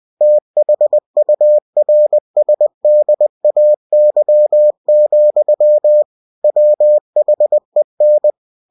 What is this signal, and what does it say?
Morse code, heard as a keyed tone, about 20 wpm, 600 Hz. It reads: THURSDAY, WHEN
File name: cw-books / winnie-the-pooh / winnie-20wpm-page-0068